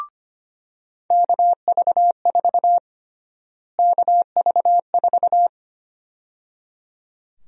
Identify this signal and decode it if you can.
Morse code, heard as a keyed tone, about 25 wpm, 700 Hz. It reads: KV4 KV4
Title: USA Call Sign Prefixes - Repeated 2x - 25wpm